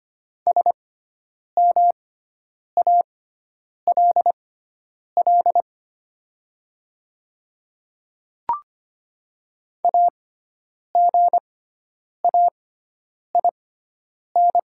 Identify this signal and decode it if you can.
Morse code, heard as a keyed tone, about 25 wpm, spaced as for 9 wpm, 700 Hz. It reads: SMALL AGAIN
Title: 6x Character Spacing - Top 200 Words - 25wpm